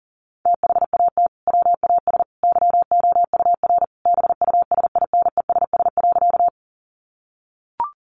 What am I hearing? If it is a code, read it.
Morse code, heard as a keyed tone, about 40 wpm, 700 Hz. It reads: THAT WAS YOUR BUSINESS.
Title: Sentences from Top 600 Words 40wpm